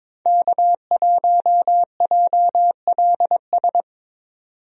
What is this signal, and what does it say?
Morse code, heard as a keyed tone, about 22 wpm, 700 Hz. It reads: K1JLS
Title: USA Call Signs - 5 Characters - 22wpm